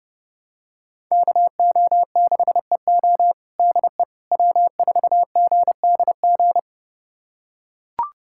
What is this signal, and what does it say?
Morse code, heard as a keyed tone, about 30 wpm, 700 Hz. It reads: KO6EO DE W4GDG
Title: USA Callsign DE Callsign - 5 Characters - 30wpm